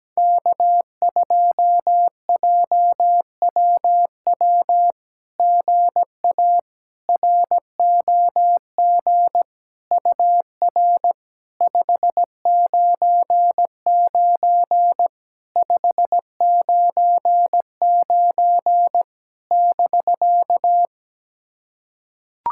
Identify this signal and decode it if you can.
Morse code, heard as a keyed tone, about 17 wpm, 700 Hz. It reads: K2JWW GA ROG UR 599 599 <BK>
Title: SOTA QSO Practice 17wpm